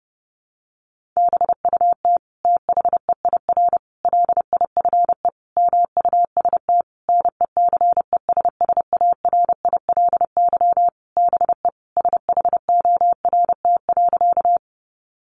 Morse code, heard as a keyed tone, about 30 wpm, 700 Hz. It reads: BUT THEIR LIFE MUST NECESSARILY BE SHORT.
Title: Sentences from Top 1400 Words 30wpm